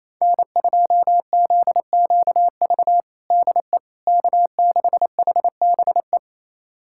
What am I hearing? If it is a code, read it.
Morse code, heard as a keyed tone, about 27 wpm, 700 Hz. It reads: N2ZQV DE K6HBE
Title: USA Callsign DE Callsign 28wpm